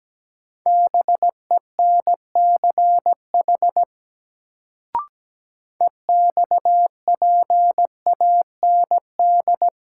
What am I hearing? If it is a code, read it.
Morse code, heard as a keyed tone, about 17 wpm, 700 Hz. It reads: BENCH EXPAND
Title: Sentences from Top 1400 Words - New Word Drill - 17wpm